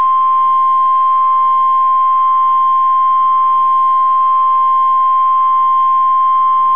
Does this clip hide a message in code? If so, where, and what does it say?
Yes: Emergency Alert System (SAME) tones (the 1050 Hz attention tone) from 1.0 to 5.8 s.